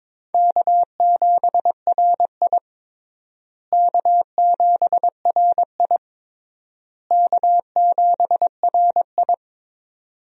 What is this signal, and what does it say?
Morse code, heard as a keyed tone, about 22 wpm, 700 Hz. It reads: K7RI K7RI K7RI
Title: USA Call Signs - 4 Characters - Repeated 3x - 22wpm